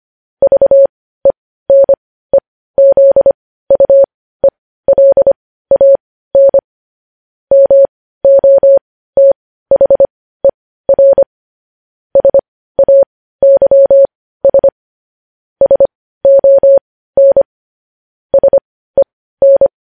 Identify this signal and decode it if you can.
Morse code, heard as a keyed tone, about 25 wpm, spaced as for 15 wpm, 550 Hz. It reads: VENEZUELAN MOTHER SAYS SON SEN
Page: news headlines in morse code at 15 wpm